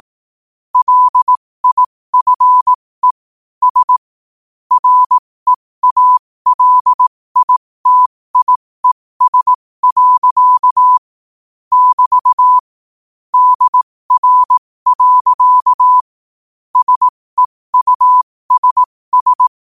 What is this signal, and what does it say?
Morse code, heard as a keyed tone, about 18 wpm, spaced as for 15 wpm, 1000 Hz. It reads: LIFE S REALITIES. <BT> DR. SEUSS
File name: qotdfiles / 15wpm